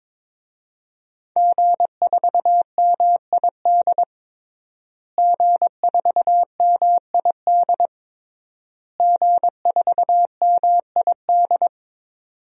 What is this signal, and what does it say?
Morse code, heard as a keyed tone, about 22 wpm, 700 Hz. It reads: G4MID G4MID G4MID
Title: DXCC-100 Call Signs - 5 Characters - Repeated 3x - 22wpm